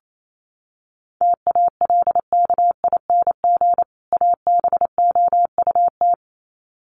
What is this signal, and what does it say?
Morse code, heard as a keyed tone, about 28 wpm, 700 Hz. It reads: TALKING ABOUT
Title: Sets of 2 Words 28wpm